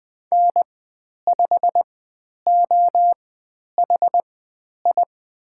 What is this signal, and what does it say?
Morse code, heard as a keyed tone, about 20 wpm, spaced as for 10 wpm, 700 Hz. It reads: N5OHI
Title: 095 Lesson 16g - Letter c - Call Signs 20wpm